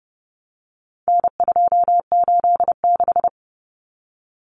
Morse code, heard as a keyed tone, about 30 wpm, 700 Hz. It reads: N286
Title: Common Cut Numbers - 4-Digit 30wpm